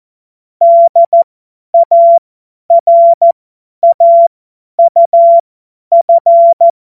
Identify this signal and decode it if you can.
Morse code, heard as a keyed tone, about 14 wpm, spaced as for 10 wpm, 680 Hz. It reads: DARAUF
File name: Morsekurs / Witze / Witz 05 Dezember 2023 50 F70